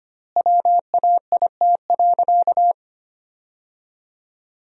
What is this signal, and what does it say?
Morse code, heard as a keyed tone, about 25 wpm, 700 Hz. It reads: WAIT.
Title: 3x Word-Spacing - Sentences from Top 500 Words - 25wpm